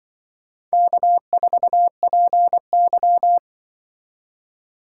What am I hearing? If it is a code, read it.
Morse code, heard as a keyed tone, about 24 wpm, 700 Hz. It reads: K4PY
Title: Speed Racing - USA 1x2 Call Signs 20wpm